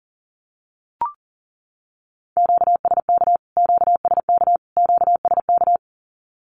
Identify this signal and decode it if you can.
Morse code, heard as a keyed tone, about 38 wpm, 700 Hz. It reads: QSK QSK QSK
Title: Q-Codes Repeated 3x 40wpm